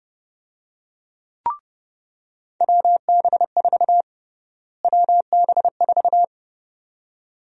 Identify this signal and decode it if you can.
Morse code, heard as a keyed tone, about 31 wpm, 700 Hz. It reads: WB4 WB4
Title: USA Call Sign Prefixes - Repeated 2x - 30wpm